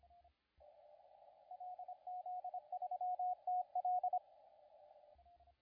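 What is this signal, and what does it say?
Morse code, heard as a keyed tone, about 26 wpm, 730 Hz, with radio noise. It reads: LZ3TL
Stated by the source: CW contest audio recorded at LA4C during the SAC CW contest.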